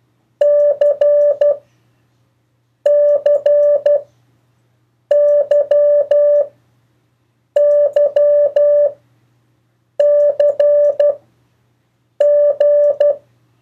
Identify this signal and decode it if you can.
Morse code, heard as a keyed tone, about 12 wpm, 580 Hz. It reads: C C Y Y C G
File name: 1of4-cw-letters